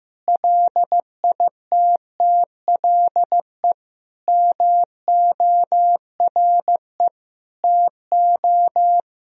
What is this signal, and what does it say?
Morse code, heard as a keyed tone, about 16 wpm, 700 Hz. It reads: LITTLE MORE TO
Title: Sentences from Top 1000 Words - Review Part 1 - 15wpm